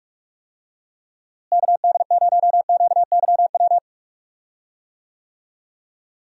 Morse code, heard as a keyed tone, about 43 wpm, 700 Hz. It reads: KD0QYW
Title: USA Call Signs - 6 Characters - Repeated 2x - 45wpm